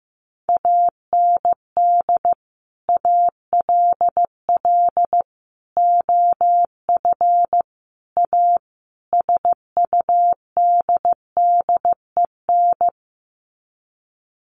Morse code, heard as a keyed tone, about 15 wpm, 700 Hz. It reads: AND ALL OF A SUDDEN
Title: Sets of 5 Words 15wpm